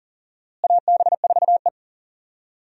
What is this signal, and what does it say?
Morse code, heard as a keyed tone, about 40 wpm, 700 Hz. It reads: AB4E
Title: USA Call Signs - 4 Characters - Repeated 3x - 40wpm